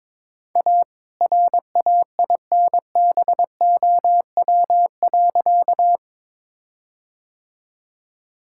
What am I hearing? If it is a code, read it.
Morse code, heard as a keyed tone, about 21 wpm, 700 Hz. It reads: A RAINBOW.
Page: morse-code-ninja-practice-22wpm